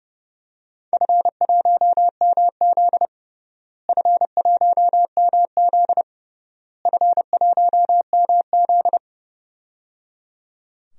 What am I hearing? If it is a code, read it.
Morse code, heard as a keyed tone, about 30 wpm, 700 Hz. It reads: F1MZ F1MZ F1MZ
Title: DXCC-100 Call Signs - 4 Characters - Repeated 3x - 30wpm